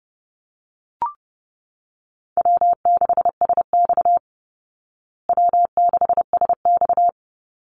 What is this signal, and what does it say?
Morse code, heard as a keyed tone, about 30 wpm, 700 Hz. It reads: W6SX W6SX
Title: Easy Call Signs Repeated 3x - 30wpm